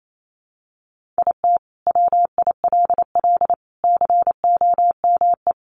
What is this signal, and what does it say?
Morse code, heard as a keyed tone, about 28 wpm, 700 Hz. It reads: IT WILL COME
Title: Sentences from Top 1400 Words 28wpm